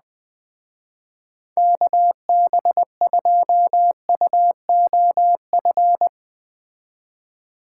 Morse code, heard as a keyed tone, about 21 wpm, 700 Hz. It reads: KB2UOF